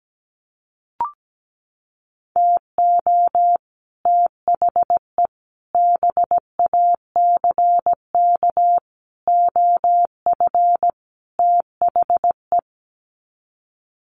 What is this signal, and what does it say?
Morse code, heard as a keyed tone, about 17 wpm, 700 Hz. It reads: TO THE BACK OF THE
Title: Sets of 5 Words 17wpm